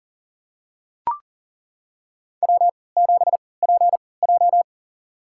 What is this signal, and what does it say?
Morse code, heard as a keyed tone, about 40 wpm, spaced as for 23 wpm, 700 Hz. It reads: W7PJ
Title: USA Call Signs - 4 Characters - 3x Character Spacing - 40wpm